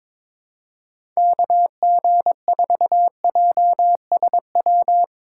Morse code, heard as a keyed tone, about 22 wpm, 700 Hz. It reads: KG4JSW
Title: USA Call Signs - 6 Characters - 4x Character Spacing - 22wpm